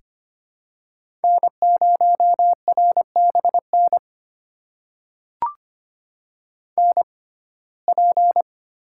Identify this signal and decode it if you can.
Morse code, heard as a keyed tone, about 25 wpm, spaced as for 18 wpm, 700 Hz. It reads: N0RBN NP
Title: USA Call Signs - 5 Characters - 6x Character Spacing - 25wpm